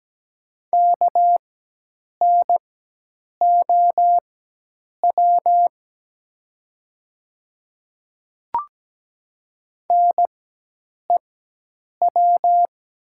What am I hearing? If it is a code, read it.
Morse code, heard as a keyed tone, about 17 wpm, spaced as for 8 wpm, 700 Hz. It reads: KNOW NEW
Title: 4x Character Spacing - Top 300 Words - 17wpm